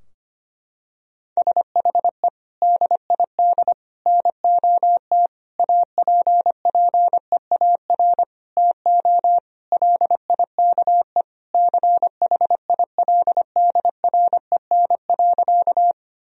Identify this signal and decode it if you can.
Morse code, heard as a keyed tone, about 25 wpm, 700 Hz. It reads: SHE DID NOT APPEAR TO LIKE CHILDREN.
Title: Sentences from Top 1000 Words - Review Part 2 - 25wpm